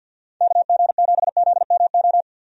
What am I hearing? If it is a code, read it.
Morse code, heard as a keyed tone, about 51 wpm, 700 Hz. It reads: KG7ZMO
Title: USA Call Signs - 6 Characters - 50wpm